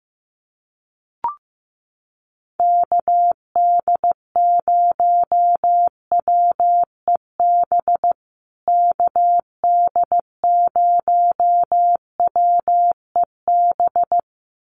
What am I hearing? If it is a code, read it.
Morse code, heard as a keyed tone, about 14 wpm, 700 Hz. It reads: KD0WEB KD0WEB
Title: USA Call Signs Repeated 2x 15wpm